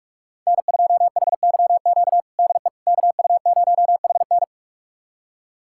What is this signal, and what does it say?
Morse code, heard as a keyed tone, about 48 wpm, 700 Hz. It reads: N2HYQ DE KU0HN
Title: USA Callsign DE Callsign - 5 Characters - 45wpm